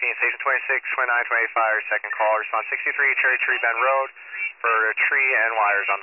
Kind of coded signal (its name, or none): Emergency Alert System (SAME) tones